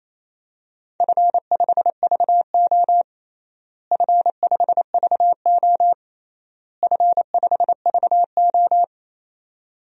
Morse code, heard as a keyed tone, about 27 wpm, 700 Hz. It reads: F5VO F5VO F5VO
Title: DXCC-100 Call Signs - 4 Characters - Repeated 3x - 28wpm